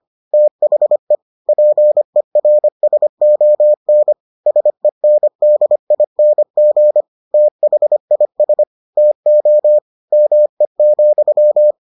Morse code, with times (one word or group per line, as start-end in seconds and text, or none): 0.3-1.1 THE
1.5-4.1 PERSON
4.5-7.0 SENDING
7.3-8.6 THIS
9.0-9.8 TO
10.1-11.7 ME,